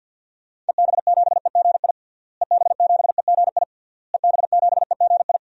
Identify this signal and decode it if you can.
Morse code, heard as a keyed tone, about 47 wpm, 700 Hz. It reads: EB7EGI EB7EGI EB7EGI
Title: DXCC-100 Call Signs - 6 Characters - Repeated 3x - 50wpm